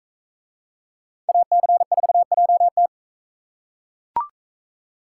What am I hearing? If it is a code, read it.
Morse code, heard as a keyed tone, about 42 wpm, 700 Hz. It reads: AC4JT